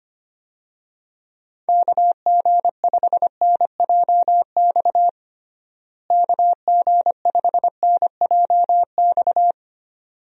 Morse code, heard as a keyed tone, about 25 wpm, 700 Hz. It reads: KG5NJX KG5NJX
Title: USA Call Signs - 6 Characters - Repeated 2x - 25wpm